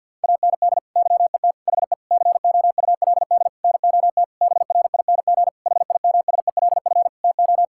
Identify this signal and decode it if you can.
Morse code, heard as a keyed tone, about 50 wpm, 700 Hz. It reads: AND YET HE COULD NOT BRING HIMSELF TO
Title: Sentences from Top 1600 Words 50wpm